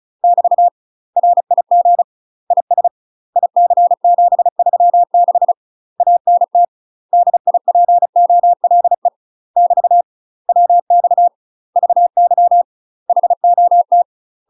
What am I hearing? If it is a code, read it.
Morse code, heard as a keyed tone, about 36 wpm, 700 Hz. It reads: <BT> RIG IS IC736 ANT DIPOLE <BT> WX VY HOT
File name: UK MASTER QSOs TRAINING / 35WPM / UK MASTER QSOs 0014